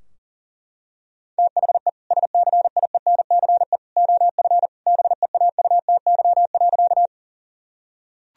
Morse code, heard as a keyed tone, about 40 wpm, 700 Hz. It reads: THE SCIENCE OF BEAUTY.
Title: Sentences from Top 1000 Words - Review Part 2 - 40wpm